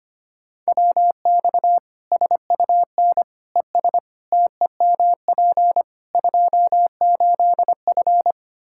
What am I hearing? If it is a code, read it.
Morse code, heard as a keyed tone, about 25 wpm, 700 Hz. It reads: WX SUN ES TEMP 28F
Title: Practice QSO - Exchange 2 - Info - 25wpm